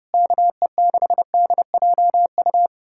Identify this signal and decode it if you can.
Morse code, heard as a keyed tone, about 33 wpm, 700 Hz. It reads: KE6DJU